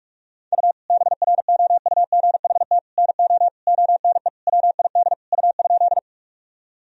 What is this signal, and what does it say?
Morse code, heard as a keyed tone, about 41 wpm, 700 Hz. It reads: U BROUGHT NO ONE WID U?
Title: Sentences from Top 500 Words and Abbreviations 45wpm